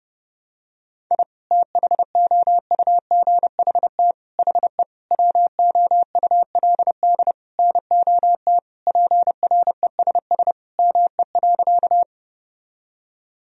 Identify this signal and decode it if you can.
Morse code, heard as a keyed tone, about 30 wpm, 700 Hz. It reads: I THOUGHT HE WOULD NOT PRESS ME.
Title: Sentences from Top 1000 Words - Review Part 1 - 30wpm